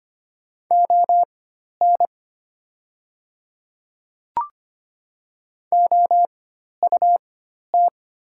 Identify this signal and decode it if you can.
Morse code, heard as a keyed tone, about 25 wpm, spaced as for 12 wpm, 700 Hz. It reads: ON OUT
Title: 4x Character Spacing - Top 100 Words - 25wpm